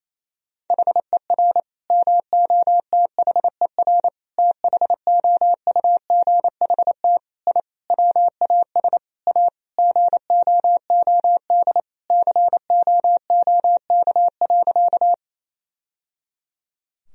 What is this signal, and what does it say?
Morse code, heard as a keyed tone, about 28 wpm, 700 Hz. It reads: HER MOTHER THOUGHT I WAS A GOOD COOK.
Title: Sentences from Top 1000 Words - Review Part 4 - 28wpm